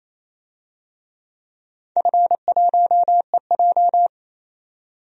Morse code, heard as a keyed tone, about 27 wpm, 700 Hz. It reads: F1EJ
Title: DXCC-100 Call Signs - 4 Characters - 28wpm